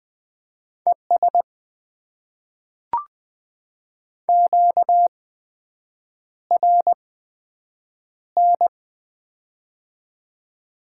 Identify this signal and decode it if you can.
Morse code, heard as a keyed tone, about 20 wpm, spaced as for 7 wpm, 700 Hz. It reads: ES QRN
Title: Top 90 QSO Elements - 8x Character Spacing - 20wpm